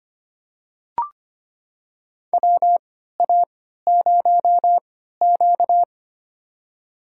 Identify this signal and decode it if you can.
Morse code, heard as a keyed tone, about 25 wpm, spaced as for 14 wpm, 700 Hz. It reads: WA0Q
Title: USA Call Signs - 4 Characters - 3x Character Spacing - 25wpm